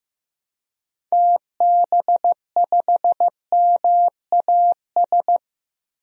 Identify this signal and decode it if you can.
Morse code, heard as a keyed tone, about 15 wpm, 700 Hz. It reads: TB5MAS
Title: DXCC-100 Call Signs - 6 Characters - 15wpm